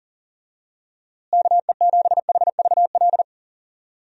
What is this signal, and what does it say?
Morse code, heard as a keyed tone, about 40 wpm, 700 Hz. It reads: KE7HVL
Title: USA Call Signs - 6 Characters - Repeated 2x - 40wpm